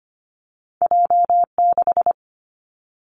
Morse code, heard as a keyed tone, about 25 wpm, 700 Hz. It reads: J 6
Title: Two Challenging Letter-Number Combinations 25wpm